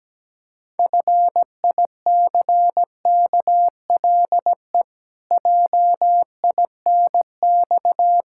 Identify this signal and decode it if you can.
Morse code, heard as a keyed tone, about 17 wpm, 700 Hz. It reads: FICKLE JINX
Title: Pangram Challenge v2 - 17wpm